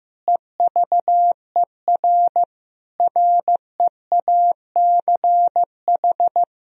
Morse code, heard as a keyed tone, about 15 wpm, 700 Hz. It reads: EVER REACH
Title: Sentences from Top 1000 Words - Review Part 2 - 15wpm